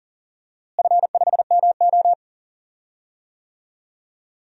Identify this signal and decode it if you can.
Morse code, heard as a keyed tone, about 40 wpm, 700 Hz. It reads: F5MO
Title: DXCC-100 Call Signs - 4 Characters - Repeated 2x - 40wpm